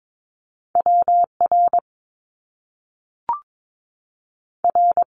Morse code, heard as a keyed tone, about 22 wpm, spaced as for 3 wpm, 700 Hz. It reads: W RR
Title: Top Two Letter Combinations in Words 22wpm